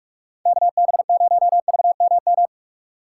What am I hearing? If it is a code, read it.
Morse code, heard as a keyed tone, about 35 wpm, 700 Hz. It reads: KB0VMM
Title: Speed Racing - USA Call Signs 30wpm